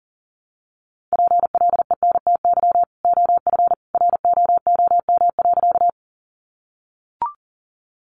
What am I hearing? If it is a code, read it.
Morse code, heard as a keyed tone, about 40 wpm, 700 Hz. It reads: PLENTY OF ROOM.
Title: Sentences from Top 1300 Words 40wpm